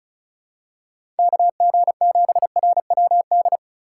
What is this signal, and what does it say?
Morse code, heard as a keyed tone, about 36 wpm, 700 Hz. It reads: KG7RWD
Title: USA Call Signs - 2x Character Spacing - 35wpm